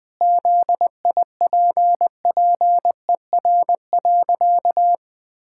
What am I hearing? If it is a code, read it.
Morse code, heard as a keyed tone, about 20 wpm, 700 Hz. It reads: ZIPPER.
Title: Pangram Challenge v2 - 20wpm